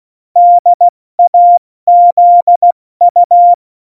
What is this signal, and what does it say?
Morse code, heard as a keyed tone, about 16 wpm, 700 Hz. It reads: DAZU
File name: Morsekurs / Witze / Witz 08. Januar 2019, 70 F80